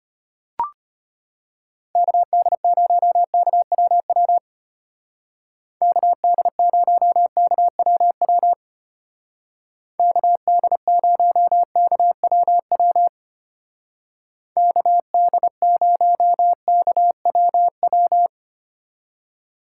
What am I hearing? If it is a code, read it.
Morse code, heard as a keyed tone, about 32 wpm, 700 Hz. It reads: KD0KWW KD0KWW KD0KWW KD0KWW